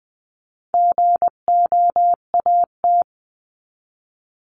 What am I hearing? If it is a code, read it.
Morse code, heard as a keyed tone, about 20 wpm, 700 Hz. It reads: GOAT
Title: 259 Lesson 42d - Words 18-20wpm